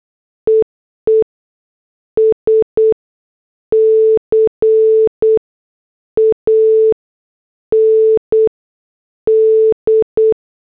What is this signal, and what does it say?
Morse code, heard as a keyed tone, about 8 wpm, spaced as for 6 wpm, 440 Hz. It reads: EE SCAND
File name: morse / headlines